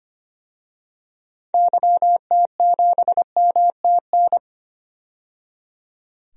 Morse code, heard as a keyed tone, about 25 wpm, 700 Hz. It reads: YT7MTN
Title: DXCC-100 Call Signs - 6 Characters - 25wpm